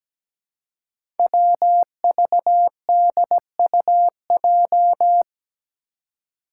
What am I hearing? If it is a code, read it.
Morse code, heard as a keyed tone, about 17 wpm, 700 Hz. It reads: WVDUJ
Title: Five Challenging Letter-Number Combinations 17wpm